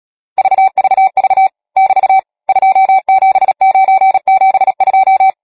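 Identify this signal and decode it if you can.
Morse code, heard as a keyed tone, about 37 wpm, 770 Hz. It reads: VVV <BT> 27972
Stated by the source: Morse code number practice, machine-sent training text (Nácvik telegrafie, OK1FCB).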